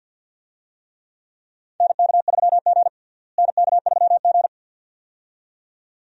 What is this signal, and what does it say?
Morse code, heard as a keyed tone, about 51 wpm, 700 Hz. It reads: NK3G NK3G
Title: USA Call Signs - 4 Characters - Repeated 2x - 50wpm